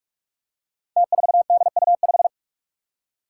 Morse code, heard as a keyed tone, about 44 wpm, 700 Hz. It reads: T4DU5
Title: Five Challenging Letter-Number Combinations 45wpm